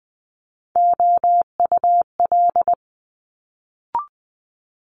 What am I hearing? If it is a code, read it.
Morse code, heard as a keyed tone, about 20 wpm, 700 Hz. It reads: OUL
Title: Multi-letter Phonemes Quick Fire 20wpm